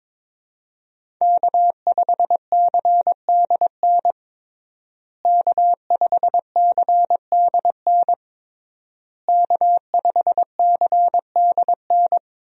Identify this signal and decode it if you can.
Morse code, heard as a keyed tone, about 22 wpm, 700 Hz. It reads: K5CDN K5CDN K5CDN
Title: USA Call Signs - 5 Characters - Repeated 3x - 22wpm